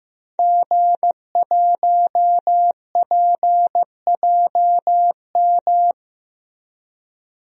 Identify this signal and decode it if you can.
Morse code, heard as a keyed tone, about 15 wpm, 700 Hz. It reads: G1PJM
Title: DXCC-100 Call Signs - 5 Characters - Repeated 3x - 15wpm